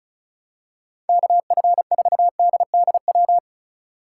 Morse code, heard as a keyed tone, about 36 wpm, 700 Hz. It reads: KF4DDW
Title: USA Call Signs - 6 Characters - 2x Character Spacing - 35wpm